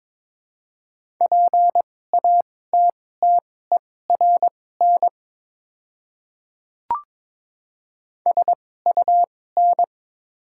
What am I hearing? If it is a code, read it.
Morse code, heard as a keyed tone, about 22 wpm, spaced as for 16 wpm, 700 Hz. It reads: PATTERN SUN
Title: 2x Character Spacing - Top 500 Words - 22wpm